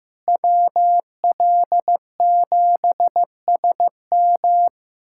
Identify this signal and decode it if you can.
Morse code, heard as a keyed tone, about 15 wpm, 700 Hz. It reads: WL7SM
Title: USA Callsign DE Callsign - 5 Characters - 15wpm